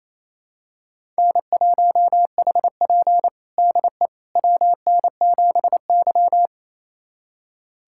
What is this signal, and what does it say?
Morse code, heard as a keyed tone, about 28 wpm, 690 Hz. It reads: N1HP DE WN7Y
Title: USA Callsign DE Callsign - 4 Characters - 28wpm